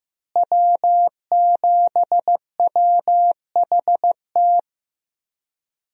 Morse code, heard as a keyed tone, about 15 wpm, 700 Hz. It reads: W7WHT
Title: USA Call Signs - 5 Characters - Repeated 3x - 15wpm